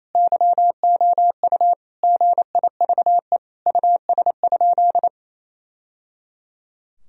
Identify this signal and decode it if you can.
Morse code, heard as a keyed tone, about 28 wpm, 700 Hz. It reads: YOU GIVE US?
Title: Sentences from Top 1000 Words - Review Part 4 - 28wpm